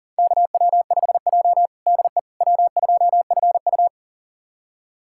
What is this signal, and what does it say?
Morse code, heard as a keyed tone, about 40 wpm, 700 Hz. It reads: KW5J DE W2FU
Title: USA Callsign DE Callsign - 4 Characters - 40wpm